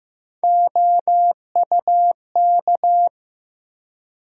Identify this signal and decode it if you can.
Morse code, heard as a keyed tone, about 15 wpm, 700 Hz. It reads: OUK
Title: USA Call Sign Postfixes - 3 Letters 15wpm